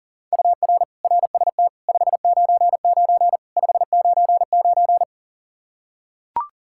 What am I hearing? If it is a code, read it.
Morse code, heard as a keyed tone, about 40 wpm, 700 Hz. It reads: UR RST 599 599
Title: Practice QSO - Exchange 1 - Beginning - 40wpm